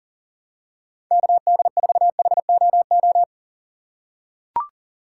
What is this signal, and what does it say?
Morse code, heard as a keyed tone, about 40 wpm, 700 Hz. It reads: KD4HOO
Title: USA Call Signs - 6 Characters - 3x Character Spacing - 40wpm